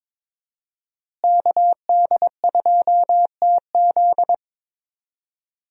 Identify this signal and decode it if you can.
Morse code, heard as a keyed tone, about 22 wpm, 700 Hz. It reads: KD2TZ
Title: USA Call Signs - 5 Characters - Repeated 3x - 22wpm